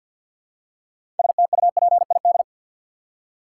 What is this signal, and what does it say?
Morse code, heard as a keyed tone, about 49 wpm, 700 Hz. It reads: STUPID
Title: Six Letter Words - 50wpm